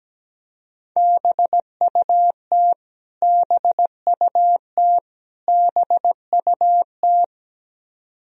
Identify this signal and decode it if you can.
Morse code, heard as a keyed tone, about 17 wpm, 700 Hz. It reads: BUT BUT BUT
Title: Top 100 Words Repeated 17wpm